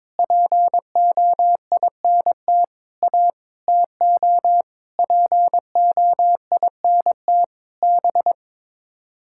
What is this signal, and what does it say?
Morse code, heard as a keyed tone, about 21 wpm, 700 Hz. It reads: POINT A TO POINT B
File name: practice / 22wpm / Sets of 8 Words - 22wpm